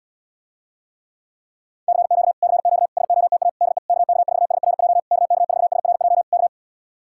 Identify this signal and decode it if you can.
Morse code, heard as a keyed tone, about 77 wpm, 700 Hz. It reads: CQ CQ SOTA DE KK4SRJ KK4SRJ K